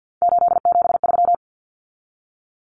Morse code, heard as a keyed tone, about 50 wpm, 700 Hz. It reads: 873
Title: Common Cut Numbers - 3-Digit 50wpm